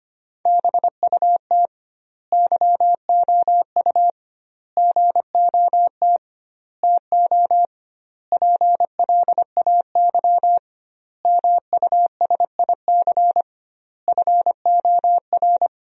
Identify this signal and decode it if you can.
Morse code, heard as a keyed tone, about 25 wpm, 700 Hz. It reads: BUT YOU GOT TO PLAY MUSIC FOR
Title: 2x Word-Spacing - Sentences from Top 300 Words - 25wpm